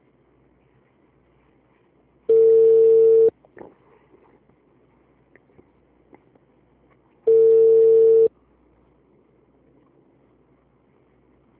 Ringback tone (China, 450 Hz).
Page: Bauru - SP